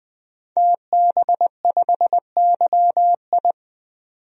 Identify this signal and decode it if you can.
Morse code, heard as a keyed tone, about 20 wpm, 700 Hz. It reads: TB5YI
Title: DXCC-100 Call Signs - 5 Characters - 20wpm